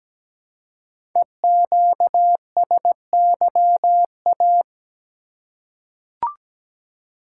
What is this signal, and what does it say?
Morse code, heard as a keyed tone, about 17 wpm, 700 Hz. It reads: EQSYA
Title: Code Groups - 5 Letters - 17wpm